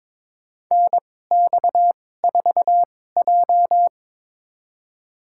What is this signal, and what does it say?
Morse code, heard as a keyed tone, about 23 wpm, spaced as for 16 wpm, 700 Hz. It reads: NX4J
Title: USA Call Signs - 4 Characters - 2x Character Spacing - 22wpm